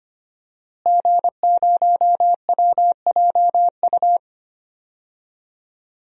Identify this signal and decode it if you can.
Morse code, heard as a keyed tone, about 25 wpm, 700 Hz. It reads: G0WJU
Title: DXCC-100 Call Signs - 5 Characters - Repeated 3x - 25wpm